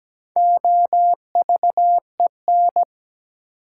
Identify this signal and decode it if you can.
Morse code, heard as a keyed tone, about 17 wpm, 700 Hz. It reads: OVEN
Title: Sentences from Top 1200 Words - New Word Drill Alt - 17wpm